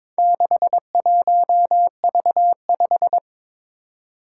Morse code, heard as a keyed tone, about 22 wpm, 700 Hz. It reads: 61V5
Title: Four Challenging Letter-Number Combinations 22wpm